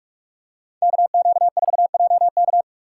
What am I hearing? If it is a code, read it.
Morse code, heard as a keyed tone, about 44 wpm, 700 Hz. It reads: KQ4JK